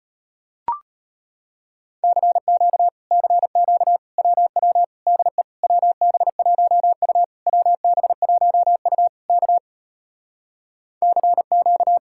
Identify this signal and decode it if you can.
Morse code, heard as a keyed tone, about 33 wpm, 700 Hz. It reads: CQ CQ WW DE WB1U WB1U K CQ